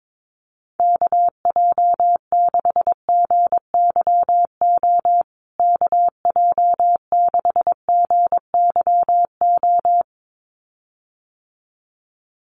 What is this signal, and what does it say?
Morse code, heard as a keyed tone, about 22 wpm, 700 Hz. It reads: KJ6GYO KJ6GYO
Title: USA Call Signs Repeated 2x 22wpm